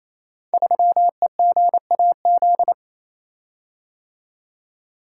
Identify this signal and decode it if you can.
Morse code, heard as a keyed tone, about 28 wpm, 700 Hz. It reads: 3EGAZ
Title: Code Groups - 5 Letter:Numbers - 28wpm